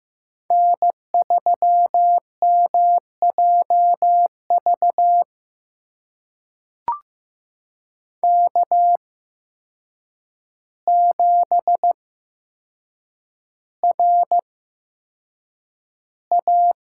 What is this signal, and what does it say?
Morse code, heard as a keyed tone, about 15 wpm, spaced as for 7 wpm, 700 Hz. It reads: N3MJV K7RA
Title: USA Call Signs - 5 Characters - 8x Character Spacing - 15wpm